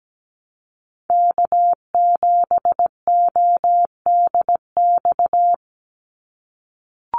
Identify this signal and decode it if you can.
Morse code, heard as a keyed tone, about 17 wpm, 700 Hz. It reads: K7ODX
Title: NA SOTA Activator Callsigns 17wpm